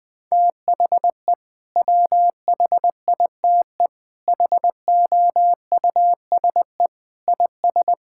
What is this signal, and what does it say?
Morse code, heard as a keyed tone, about 21 wpm, 700 Hz. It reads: THE WHITE HOUSE IS
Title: Sentences with 3-Word Phrases - 20wpm